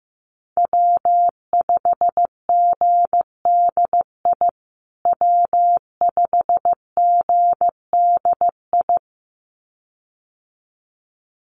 Morse code, heard as a keyed tone, about 15 wpm, 700 Hz. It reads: W5GDI W5GDI
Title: USA Call Signs Repeated 2x 15wpm